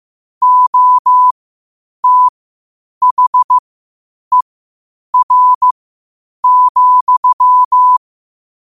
Morse code, heard as a keyed tone, about 15 wpm, spaced as for 8 wpm, 1000 Hz. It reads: OTHER,
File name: qotdfiles / 8wpm